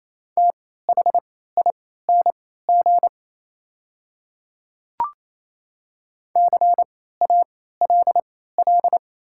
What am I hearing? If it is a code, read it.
Morse code, heard as a keyed tone, about 28 wpm, spaced as for 16 wpm, 690 Hz. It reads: THING CALL
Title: 3x Character Spacing - Top 100 Words - 28wpm